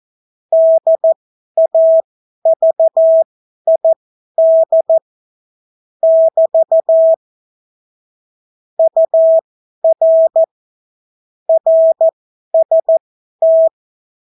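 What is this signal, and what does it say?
Morse code, heard as a keyed tone, about 14 wpm, spaced as for 11 wpm, 650 Hz. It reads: DAVID <BT> UR RST
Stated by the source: Example Spanish CW QSO from a Morse learning book, 14 wpm characters with 8 wpm spacing.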